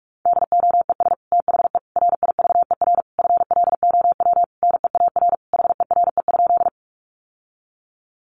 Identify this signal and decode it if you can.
Morse code, heard as a keyed tone, about 44 wpm, 700 Hz. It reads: DOES THE RIVER FLOW NEAR HERE?
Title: Sentences from Top 1200 Words 45wpm